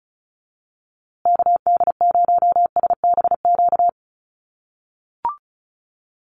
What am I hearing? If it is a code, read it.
Morse code, heard as a keyed tone, about 34 wpm, 700 Hz. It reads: KD0SBQ
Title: USA Call Signs Repeated 3x 35wpm